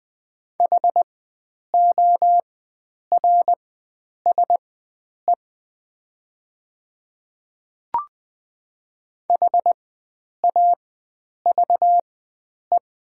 Morse code, heard as a keyed tone, about 20 wpm, spaced as for 9 wpm, 700 Hz. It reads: HORSE HAVE
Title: 4x Character Spacing - Top 500 Words - 20wpm